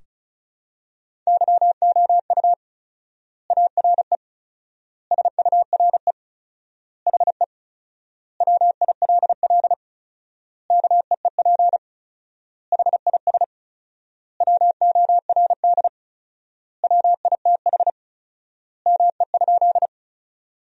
Morse code, heard as a keyed tone, about 36 wpm, 700 Hz. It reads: YOU ARE SURE HE WILL KEEP HIS WORD WITH ME?
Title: 4x Word-Spacing - Sentences from Top 300 Words - 35wpm